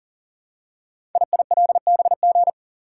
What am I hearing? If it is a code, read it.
Morse code, heard as a keyed tone, about 40 wpm, 700 Hz. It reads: IILBG
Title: Code Groups - 5 Letters - 40wpm